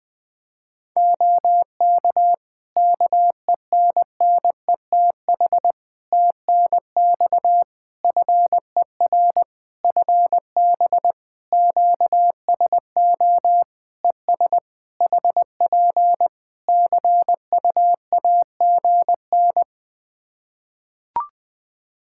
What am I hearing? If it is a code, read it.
Morse code, heard as a keyed tone, about 20 wpm, 700 Hz. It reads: OK KENNETH TNX FER FB QSO ES HP CUAGN
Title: Practice QSO - Exchange 4 - Ending - 20wpm